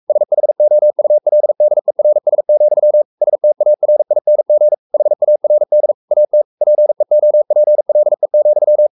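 Morse code, heard as a keyed tone, about 44 wpm, 600 Hz. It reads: SHOULDERS, STARING HARD AT PEOPLE,
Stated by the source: Morse code at 43 wpm sending Ken Follett's novel Never in English, from CW med Gnister.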